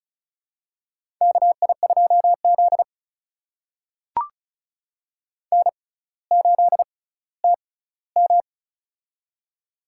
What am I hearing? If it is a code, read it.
Morse code, heard as a keyed tone, about 36 wpm, spaced as for 18 wpm, 700 Hz. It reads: KI2Z N8TM
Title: USA Call Signs - 4 Characters - 6x Character Spacing - 35wpm